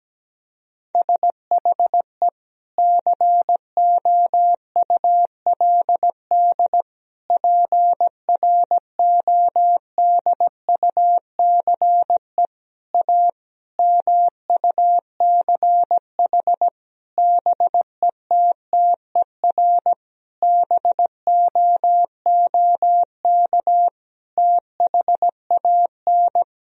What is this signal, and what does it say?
Morse code, heard as a keyed tone, about 17 wpm, 700 Hz. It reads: SHE COULD PRODUCE A MUCH BETTER BOOK THAN
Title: Sentences from Top 1000 Words - Review Part 3 - 17wpm